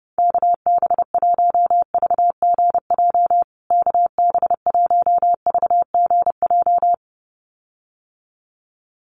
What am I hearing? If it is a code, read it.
Morse code, heard as a keyed tone, about 30 wpm, 700 Hz. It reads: KB1VGJ KB1VGJ
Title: USA Call Signs Repeated 2x 30wpm